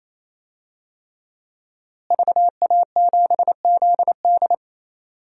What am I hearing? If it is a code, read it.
Morse code, heard as a keyed tone, about 27 wpm, 700 Hz. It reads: VA7ZD
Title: DXCC-100 Call Signs - 5 Characters - Repeated 3x - 28wpm